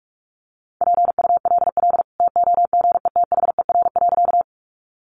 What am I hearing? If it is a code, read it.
Morse code, heard as a keyed tone, about 44 wpm, 700 Hz. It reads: PULL TOGETHER.
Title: Sentences from Top 400 Words 45wpm